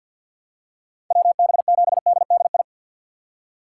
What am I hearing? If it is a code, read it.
Morse code, heard as a keyed tone, about 52 wpm, 700 Hz. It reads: WB7DDI
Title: USA Call Signs - 3x Character Spacing - 50wpm